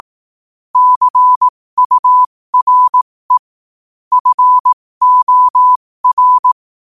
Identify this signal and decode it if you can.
Morse code, heard as a keyed tone, about 18 wpm, spaced as for 15 wpm, 1000 Hz. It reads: CURE FOR
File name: qotdfiles / 15wpm